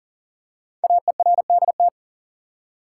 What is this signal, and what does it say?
Morse code, heard as a keyed tone, about 39 wpm, 700 Hz. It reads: AERDT
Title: Code Groups - 5 Letters - 40wpm